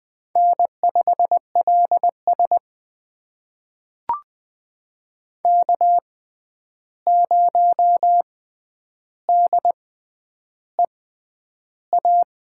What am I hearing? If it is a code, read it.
Morse code, heard as a keyed tone, about 20 wpm, spaced as for 10 wpm, 700 Hz. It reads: N5LS K0DEA
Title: USA Call Signs - 6x Character Spacing - 20wpm